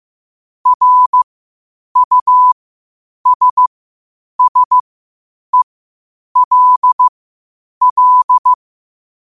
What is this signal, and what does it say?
Morse code, heard as a keyed tone, about 15 wpm, spaced as for 8 wpm, 1000 Hz. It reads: RUSSELL
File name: qotdfiles / 8wpm